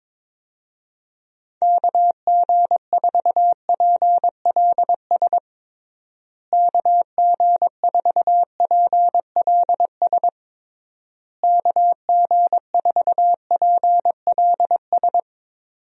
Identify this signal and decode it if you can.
Morse code, heard as a keyed tone, about 22 wpm, 700 Hz. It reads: KG4PLS KG4PLS KG4PLS
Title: USA Call Signs - 6 Characters - Repeated 3x - 22wpm